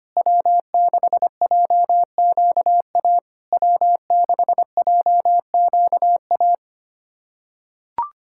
Morse code, heard as a keyed tone, about 25 wpm, 690 Hz. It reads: W6JQA W6JQA
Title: SOTA QSO Practice 25wpm